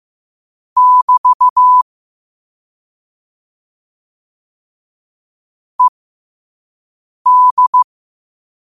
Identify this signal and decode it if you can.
Morse code, heard as a keyed tone, about 15 wpm, spaced as for 5 wpm, 1000 Hz. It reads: <BT> ED
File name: qotdfiles / 5wpm